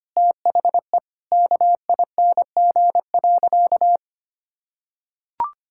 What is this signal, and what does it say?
Morse code, heard as a keyed tone, about 25 wpm, 700 Hz. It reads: THE KING.
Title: Sentences from Top 1600 Words 25wpm